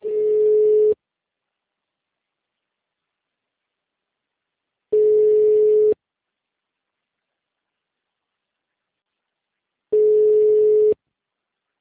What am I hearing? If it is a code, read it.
Ringback tone (Europe, 425 Hz).